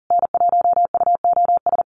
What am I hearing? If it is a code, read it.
Morse code, heard as a keyed tone, about 40 wpm, 700 Hz. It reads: N1UOS
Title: USA Call Signs Repeated 3x 40wpm